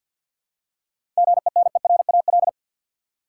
Morse code, heard as a keyed tone, about 50 wpm, 700 Hz. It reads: GENERAL
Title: Seven Letter Words - 50wpm